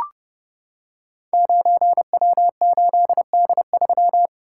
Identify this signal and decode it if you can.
Morse code, heard as a keyed tone, about 30 wpm, 700 Hz. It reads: 9W8D3
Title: Code Groups - 5 Letter:Numbers - 30wpm